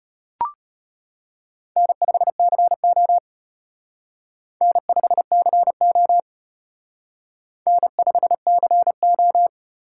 Morse code, heard as a keyed tone, about 32 wpm, 700 Hz. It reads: N5CO N5CO N5CO